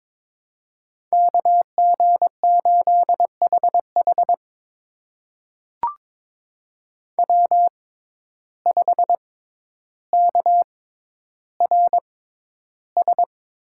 Morse code, heard as a keyed tone, about 21 wpm, spaced as for 11 wpm, 700 Hz. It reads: KG8HH W5KRS
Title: USA Call Signs - 5 Characters - 6x Character Spacing - 22wpm